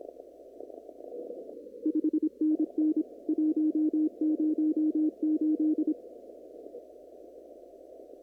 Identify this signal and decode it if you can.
Morse code, heard as a keyed tone, about 26 wpm, 310 Hz. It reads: 5NN 108
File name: CW / kt5j wpx cw-op-k5pi